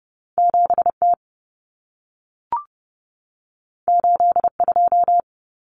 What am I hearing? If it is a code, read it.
Morse code, heard as a keyed tone, about 30 wpm, 700 Hz. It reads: 7T 82
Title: Common Cut Numbers - 2-Digit 30wpm